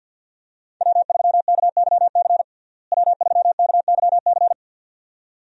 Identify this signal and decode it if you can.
Morse code, heard as a keyed tone, about 53 wpm, 700 Hz. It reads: W3KYC W3KYC
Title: USA Call Signs - 5 Characters - Repeated 2x - 50wpm